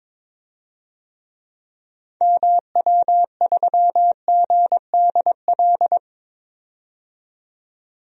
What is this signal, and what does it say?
Morse code, heard as a keyed tone, about 22 wpm, 700 Hz. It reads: MW3GDL
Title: DXCC-100 Call Signs - 6 Characters - 22wpm